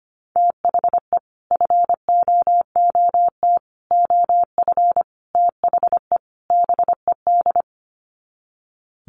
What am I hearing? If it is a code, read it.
Morse code, heard as a keyed tone, about 25 wpm, 700 Hz. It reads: THE FOOT OF THE BED
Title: Sets of 5 Words Encore2 25wpm